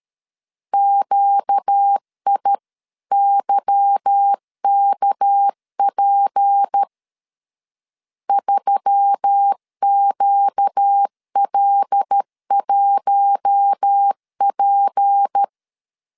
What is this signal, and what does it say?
Morse code, heard as a keyed tone, about 13 wpm, 800 Hz. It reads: QIYKP 3QL1P
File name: LEZIONI / 8 YJ45 / 8 YJ 12wpm 2235